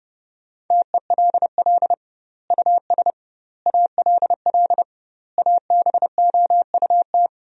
Morse code, heard as a keyed tone, about 30 wpm, 700 Hz. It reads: TELL US ALL ABOUT
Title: 2x Word-Spacing - Sentences from Top 200 Words - 30wpm